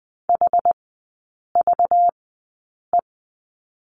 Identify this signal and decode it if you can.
Morse code, heard as a keyed tone, about 19 wpm, spaced as for 8 wpm, 700 Hz. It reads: HVE
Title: 191 Lesson 31e - Letter v - Three Characters 20wpm